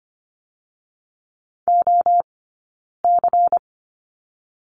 Morse code, heard as a keyed tone, about 25 wpm, 700 Hz. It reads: O C
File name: course / 25:10wpm / 092 Lesson 16d - Letter c - Two Characters 25wpm